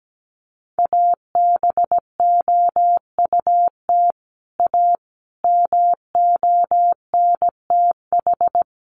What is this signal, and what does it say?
Morse code, heard as a keyed tone, about 17 wpm, 700 Hz. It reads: ABOUT A MONTH
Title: Sentences from Top 800 Words 17wpm